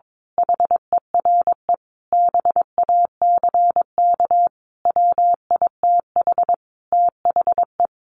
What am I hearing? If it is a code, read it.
Morse code, heard as a keyed tone, about 22 wpm, 700 Hz. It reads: HERE BACK WITH THE
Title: A Sound Of Thunder 22wpm